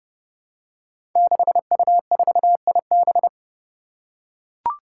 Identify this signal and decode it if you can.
Morse code, heard as a keyed tone, about 30 wpm, 700 Hz. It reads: 6U4IB
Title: Five Challenging Letter-Number Combinations 30wpm